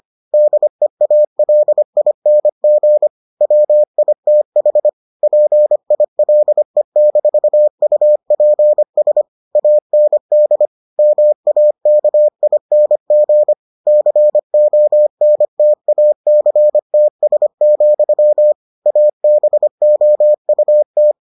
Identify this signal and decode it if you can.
Morse code, heard as a keyed tone, about 25 wpm, 600 Hz. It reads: DEALING WITH PILE-UPS AND MAKING CONTACTS, ABOUT